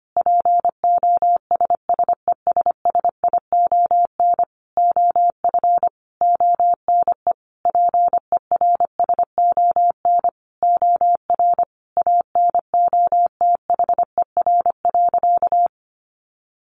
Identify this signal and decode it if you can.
Morse code, heard as a keyed tone, about 25 wpm, 700 Hz. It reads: POSSESSION OF ONE PERSON OR ANOTHER.
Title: Sentences from Top 1400 Words 25wpm